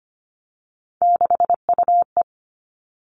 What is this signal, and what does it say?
Morse code, heard as a keyed tone, about 25 wpm, 700 Hz. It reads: 6UE
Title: All Cut Numbers - 3-Digit 25wpm